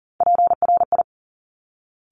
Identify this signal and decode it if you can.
Morse code, heard as a keyed tone, about 41 wpm, 700 Hz. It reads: PRI
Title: Top Three Letter Combinations in Words 40wpm